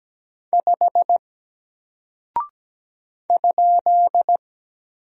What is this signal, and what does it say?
Morse code, heard as a keyed tone, about 17 wpm, 700 Hz. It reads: 5 ?
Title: Single Character (Letter:Number:Punctuation:Prosign) Alt 17wpm